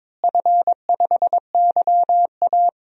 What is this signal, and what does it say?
Morse code, heard as a keyed tone, about 22 wpm, 700 Hz. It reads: F5YA
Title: DXCC-100 Call Signs - 4 Characters - Repeated 3x - 22wpm